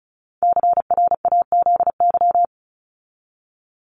Morse code, heard as a keyed tone, about 34 wpm, 700 Hz. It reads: CRAZY
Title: Top 1501-1750 Words 35wpm